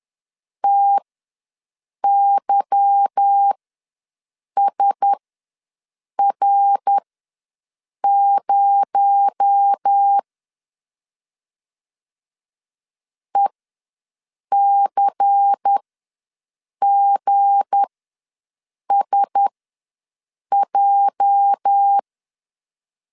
Morse code, heard as a keyed tone, about 10 wpm, spaced as for 6 wpm, 800 Hz. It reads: TYSR0 ECGSJ
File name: LEZIONI / 8 YJ45 / 8 YJ45 ws 10wpm 3843